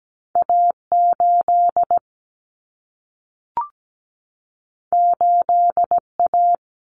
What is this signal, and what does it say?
Morse code, heard as a keyed tone, about 17 wpm, 700 Hz. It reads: A8 8A
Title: Common Cut Numbers - 2-Digit 17wpm